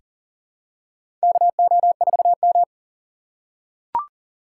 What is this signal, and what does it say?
Morse code, heard as a keyed tone, about 40 wpm, 700 Hz. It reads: KO4M
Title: USA Call Signs - 4 Characters - 3x Character Spacing - 40wpm